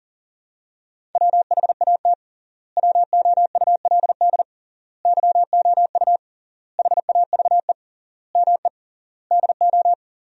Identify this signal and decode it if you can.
Morse code, heard as a keyed tone, about 40 wpm, 700 Hz. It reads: WHAT WOULD YOU HAVE ME DO
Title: 3x Word-Spacing - Sentences from Top 300 Words - 40wpm